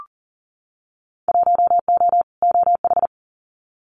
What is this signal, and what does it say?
Morse code, heard as a keyed tone, about 40 wpm, 700 Hz. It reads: 1O OH
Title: Winter Field Day Exchanges 40wpm